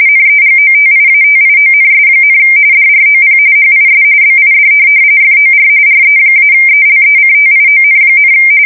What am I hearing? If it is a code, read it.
RTTY, about 45.45 baud. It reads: NEGOTIATIONS, DROP HIS BID FOR RE-ELECTION,
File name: fsk170 / USS-New-Jersey